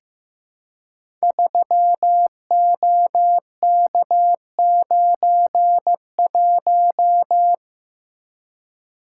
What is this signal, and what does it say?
Morse code, heard as a keyed tone, about 15 wpm, 700 Hz. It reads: 3OK91
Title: Code Groups - 5 Letter:Numbers - 15wpm